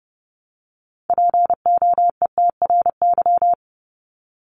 Morse code, heard as a keyed tone, about 30 wpm, 700 Hz. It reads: POETRY
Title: Top 1751-2000 Words 30wpm